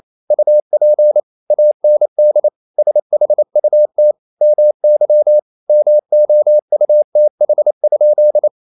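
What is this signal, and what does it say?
Morse code, heard as a keyed tone, about 28 wpm, 600 Hz. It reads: UP AND SHUT MY MOUTH?
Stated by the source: Morse code practice at 28 wpm, text of Ken Follett's novel Never in English, from CW med Gnister.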